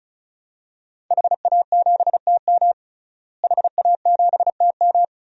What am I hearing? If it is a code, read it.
Morse code, heard as a keyed tone, about 34 wpm, 700 Hz. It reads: HA7TM HA7TM
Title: DXCC-100 Call Signs - 5 Characters - Repeated 2x - 35wpm